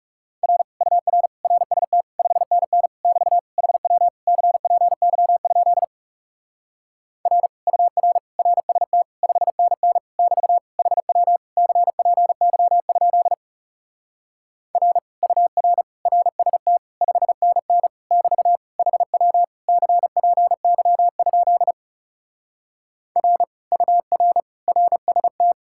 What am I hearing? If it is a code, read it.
Morse code, heard as a keyed tone, about 40 wpm, 700 Hz. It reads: R UR RST 5NN <BT> HW CPY? R UR RST 5NN <BT> HW CPY? R UR RST 5NN <BT> HW CPY? R UR RST